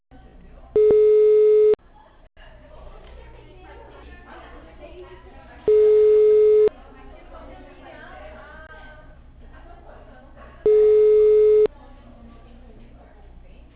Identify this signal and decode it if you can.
Ringback tone (Europe, 425 Hz), with speech.